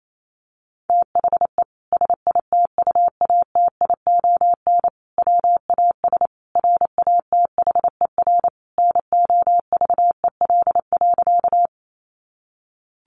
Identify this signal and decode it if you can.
Morse code, heard as a keyed tone, about 28 wpm, 700 Hz. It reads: THE SITUATION WAS RATHER NOVEL.
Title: Sentences from Top 1300 Words 28wpm